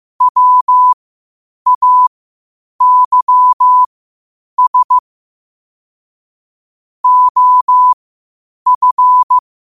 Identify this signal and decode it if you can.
Morse code, heard as a keyed tone, about 15 wpm, spaced as for 8 wpm, 1000 Hz. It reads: WAYS OF
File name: qotdfiles / 8wpm